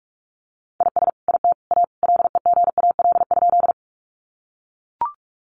Morse code, heard as a keyed tone, about 44 wpm, 700 Hz. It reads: IS IT A LEGAL?
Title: Sentences from Top 1200 Words 45wpm